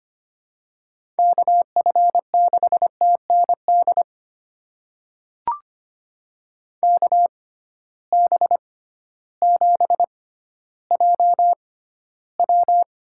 Morse code, heard as a keyed tone, about 25 wpm, spaced as for 13 wpm, 700 Hz. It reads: KF6TND KB7JW
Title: USA Call Signs - 6 Characters - 6x Character Spacing - 25wpm